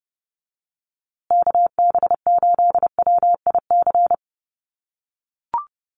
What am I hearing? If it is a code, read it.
Morse code, heard as a keyed tone, about 30 wpm, 700 Hz. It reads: KB8WIC
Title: USA Call Signs Repeated 2x 30wpm